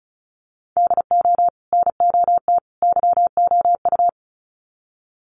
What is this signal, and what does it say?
Morse code, heard as a keyed tone, about 34 wpm, 700 Hz. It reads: DO NOT YOU
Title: Sets of 3 Words 35wpm